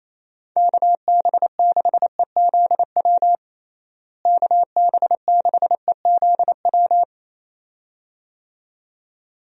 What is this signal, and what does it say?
Morse code, heard as a keyed tone, about 27 wpm, 700 Hz. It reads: KB6EZW KB6EZW
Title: USA Call Signs - 6 Characters - Repeated 2x - 28wpm